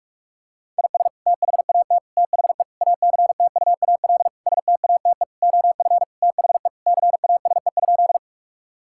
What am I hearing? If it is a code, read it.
Morse code, heard as a keyed tone, about 42 wpm, 700 Hz. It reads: IS THAT THE ACTUAL STATE OF THE CASE?
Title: Sentences from Top 1000 Words - Review Part 5 - 45wpm